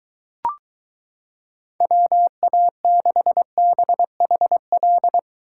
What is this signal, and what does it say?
Morse code, heard as a keyed tone, about 23 wpm, 700 Hz. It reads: WA6BHL